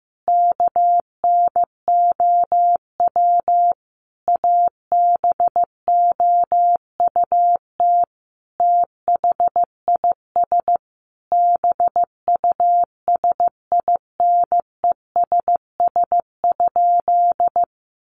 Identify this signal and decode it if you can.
Morse code, heard as a keyed tone, about 15 wpm, 700 Hz. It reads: KNOW ABOUT THIS BUSINESS?
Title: Sentences from Top 600 Words 15wpm